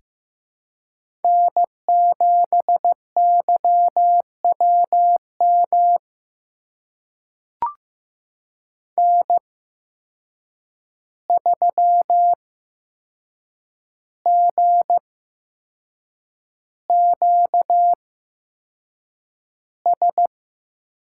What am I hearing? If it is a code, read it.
Morse code, heard as a keyed tone, about 15 wpm, spaced as for 6 wpm, 700 Hz. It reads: N7YWM N3GQS
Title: USA Call Signs - 8x Character Spacing - 15wpm